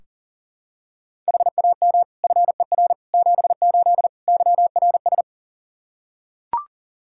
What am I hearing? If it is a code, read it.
Morse code, heard as a keyed tone, about 40 wpm, 700 Hz. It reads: HAM FER 78 YRS
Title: Practice QSO - Exchange 3 - More Info - 40wpm